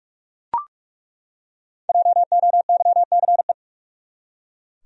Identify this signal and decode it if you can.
Morse code, heard as a keyed tone, about 41 wpm, 700 Hz. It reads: JOYCE
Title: USA Common Names - 45wpm